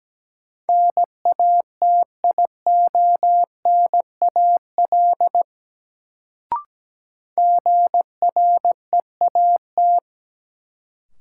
Morse code, heard as a keyed tone, about 17 wpm, 700 Hz. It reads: NATIONAL GREAT
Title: Sets of 2 Words - New Word Drill - 17wpm